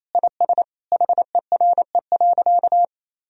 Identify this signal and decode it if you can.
Morse code, heard as a keyed tone, about 27 wpm, 700 Hz. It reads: IS HERE.
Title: Sentences from Top 1000 Words - Review Part 4 - 28wpm